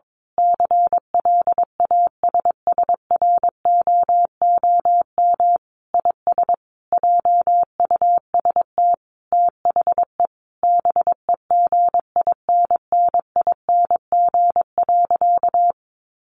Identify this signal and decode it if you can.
Morse code, heard as a keyed tone, about 22 wpm, 700 Hz. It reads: CLASSROOM IS JUST THE BEGINNING.
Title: Sentences from Top 1500 Words 22wpm